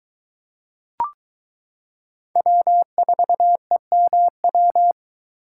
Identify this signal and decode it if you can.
Morse code, heard as a keyed tone, about 22 wpm, 700 Hz. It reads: W4EMW